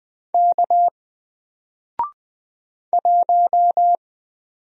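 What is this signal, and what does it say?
Morse code, heard as a keyed tone, about 20 wpm, 700 Hz. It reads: K 1
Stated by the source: Single letter and number Morse drill at 20 wpm, from N4LCD.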